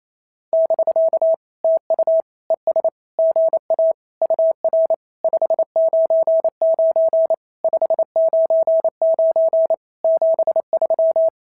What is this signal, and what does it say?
Morse code, heard as a keyed tone, about 28 wpm, 650 Hz. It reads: <BK> TU ES GA UR 599 599 73
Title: SOTA QSO Practice 28wpm